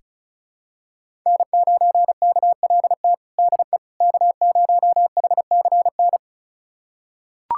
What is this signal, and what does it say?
Morse code, heard as a keyed tone, about 34 wpm, 700 Hz. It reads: N9KLT DE K0HCN
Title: USA Callsign DE Callsign - 5 Characters - 35wpm